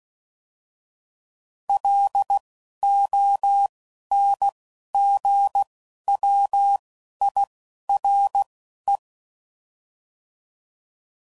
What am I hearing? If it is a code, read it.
Morse code, heard as a keyed tone, about 16 wpm, spaced as for 11 wpm, 800 Hz. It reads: LONGWIRE
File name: CW Code Groups @10WPM / fromSM's / Antennas